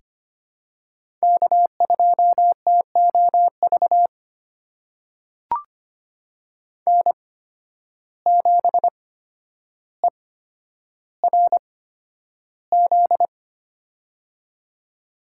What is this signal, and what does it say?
Morse code, heard as a keyed tone, about 25 wpm, spaced as for 11 wpm, 700 Hz. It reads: K2TOV N7ERZ
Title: USA Call Signs - 5 Characters - 8x Character Spacing - 25wpm